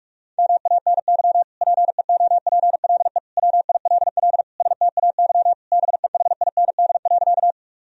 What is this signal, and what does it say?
Morse code, heard as a keyed tone, about 46 wpm, 700 Hz. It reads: MANY PEOPLE WILL STAY BEHIND.
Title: Sentences from Top 1000 Words - Review Part 1 - 45wpm